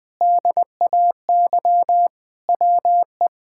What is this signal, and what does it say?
Morse code, heard as a keyed tone, about 20 wpm, 700 Hz. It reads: DAY WE
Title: Sentences with 4-Word Phrases - 20wpm